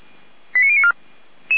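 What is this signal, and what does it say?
Five-tone selective call (ZVEI), heard as quick runs of five tones, tones of about 70 ms. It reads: 89094
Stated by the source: ZVEI five-tone call received by SDR.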